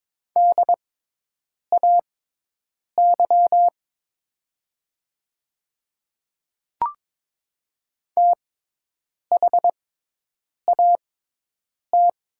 Morse code, heard as a keyed tone, about 23 wpm, spaced as for 8 wpm, 690 Hz. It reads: DAY THAT
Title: 6x Character Spacing - Top 100 Words - 22wpm